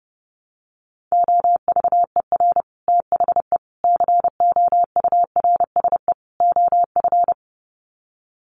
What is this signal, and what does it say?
Morse code, heard as a keyed tone, about 30 wpm, 700 Hz. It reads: OVER THE COURSE OF
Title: Sets of 4 Words 30wpm